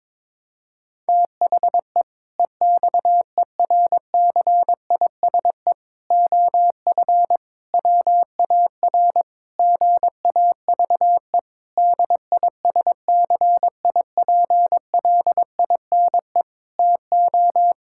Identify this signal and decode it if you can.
Morse code, heard as a keyed tone, about 22 wpm, 700 Hz. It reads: THE EXERCISE OF WAR GAVE DISCIPLINE TO
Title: Sentences from Top 1000 Words - Review Part 6 - 22wpm